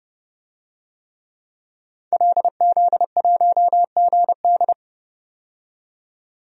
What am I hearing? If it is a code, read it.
Morse code, heard as a keyed tone, about 30 wpm, 700 Hz. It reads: LZ1GD
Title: DXCC-100 Call Signs - 5 Characters - 30wpm